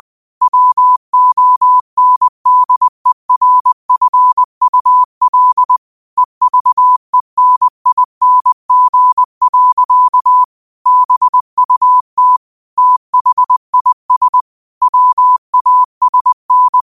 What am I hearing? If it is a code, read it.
Morse code, heard as a keyed tone, about 20 wpm, 1000 Hz. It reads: WONDERFUL EVENING. BUT THIS WASN
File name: qotdfiles / 20wpm